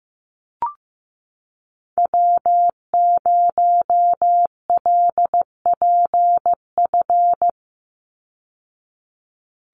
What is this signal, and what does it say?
Morse code, heard as a keyed tone, about 15 wpm, 700 Hz. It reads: W0LPF
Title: Call Signs Repeated 3x - 15wpm